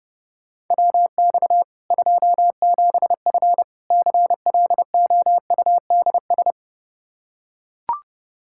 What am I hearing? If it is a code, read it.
Morse code, heard as a keyed tone, about 30 wpm, 700 Hz. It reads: WX 27F CLOUDS
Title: K1USN SST QSO Practice 30wpm